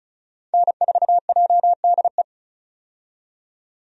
Morse code, heard as a keyed tone, about 36 wpm, 700 Hz. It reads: N4JDE
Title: USA Call Signs - 5 Characters - 35wpm